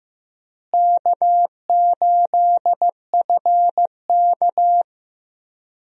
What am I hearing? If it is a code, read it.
Morse code, heard as a keyed tone, about 15 wpm, 700 Hz. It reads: K8FK
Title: USA Call Signs - 4 Characters - Repeated 2x - 15wpm